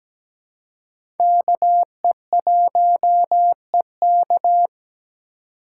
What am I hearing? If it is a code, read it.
Morse code, heard as a keyed tone, about 17 wpm, 700 Hz. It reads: KE1EK
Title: USA Call Signs - 5 Characters - 2x Character Spacing - 17wpm